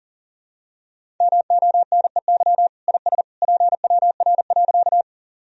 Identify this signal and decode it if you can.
Morse code, heard as a keyed tone, about 40 wpm, 700 Hz. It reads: MONEY IS PWR.
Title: Sentences from Top 500 Words and Abbreviations 40wpm